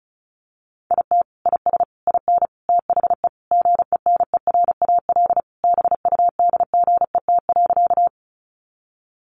Morse code, heard as a keyed tone, about 34 wpm, 700 Hz. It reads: IT IS IN THE GENERAL BUDGET.
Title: Sentences from Top 1200 Words 35wpm